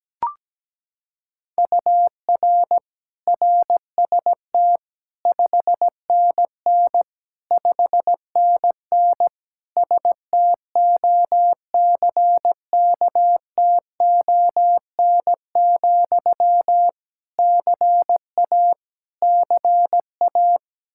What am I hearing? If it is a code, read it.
Morse code, heard as a keyed tone, about 17 wpm, 700 Hz. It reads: UR RST 5NN 5NN STOCKTON, CA CA
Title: Practice QSO - Exchange 1 - Beginning - 17wpm